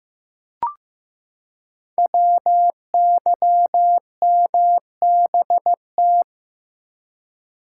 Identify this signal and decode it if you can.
Morse code, heard as a keyed tone, about 15 wpm, 700 Hz. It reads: WYMBT
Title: Code Groups - 5 Letters - 15wpm